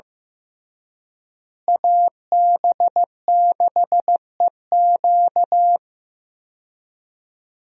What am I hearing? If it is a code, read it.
Morse code, heard as a keyed tone, about 15 wpm, 700 Hz. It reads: AB6EQ
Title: USA Call Signs - 5 Characters - Repeated 3x - 15wpm